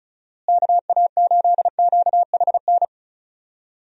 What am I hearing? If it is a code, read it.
Morse code, heard as a keyed tone, about 34 wpm, 700 Hz. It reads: KA8QHN
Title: USA Call Signs - 6 Characters - Repeated 3x - 35wpm